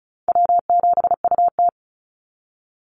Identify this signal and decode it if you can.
Morse code, heard as a keyed tone, about 34 wpm, 700 Hz. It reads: W7UT
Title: Easy Call Signs Repeated 2x - 35wpm